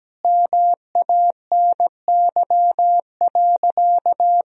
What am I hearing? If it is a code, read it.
Morse code, heard as a keyed tone, about 17 wpm, 700 Hz. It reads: MANY.
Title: Sentences from Top 1000 Words - Review Part 1 - 17wpm